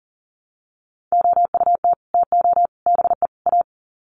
Morse code, heard as a keyed tone, about 40 wpm, 700 Hz. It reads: OUT TO BE A
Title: Sets of 4 Words 40wpm